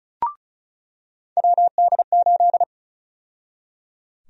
Morse code, heard as a keyed tone, about 36 wpm, 700 Hz. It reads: WD8
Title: USA Call Sign Prefixes - 35wpm